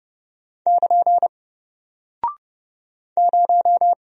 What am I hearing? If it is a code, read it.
Morse code, heard as a keyed tone, about 30 wpm, 700 Hz. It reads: <KN> 0
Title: Single Character (Letter:Number:Punctuation:Prosign) Alt 30wpm